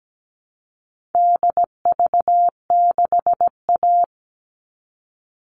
Morse code, heard as a keyed tone, about 17 wpm, 700 Hz. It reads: DV6A
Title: All Cut Numbers - 4-Digit 17wpm